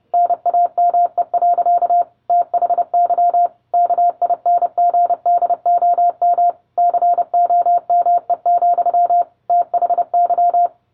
Morse code, heard as a keyed tone, about 30 wpm, 700 Hz. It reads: NAME. THY KINGDOM COME, THY